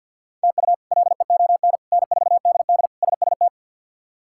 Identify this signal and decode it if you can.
Morse code, heard as a keyed tone, about 50 wpm, 700 Hz. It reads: TU LEON N4DD SST
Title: K1USN SST QSO Practice - 50wpm